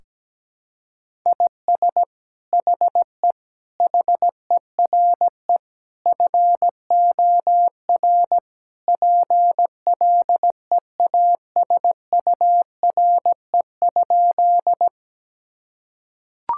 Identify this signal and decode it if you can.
Morse code, heard as a keyed tone, about 17 wpm, 700 Hz. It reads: IS HE HERE FOR PLEASURE?
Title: Sentences from Top 1600 Words 17wpm